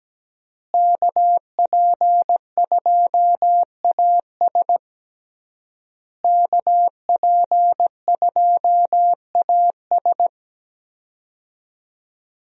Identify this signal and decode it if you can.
Morse code, heard as a keyed tone, about 17 wpm, 700 Hz. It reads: KP2AS KP2AS
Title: USA Call Signs - 5 Characters - Repeated 3x - 17wpm